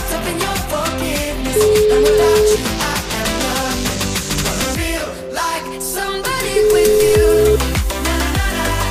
Ringback tone (Europe, 425 Hz), with noise.